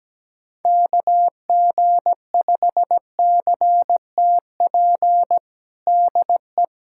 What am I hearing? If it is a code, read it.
Morse code, heard as a keyed tone, about 17 wpm, 700 Hz. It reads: KG5CTP DE
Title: USA Callsign DE Callsign - 6 Characters - 17wpm